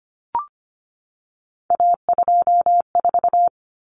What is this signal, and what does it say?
Morse code, heard as a keyed tone, about 25 wpm, 700 Hz. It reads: A24
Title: Common Cut Numbers - 3-Digit 25wpm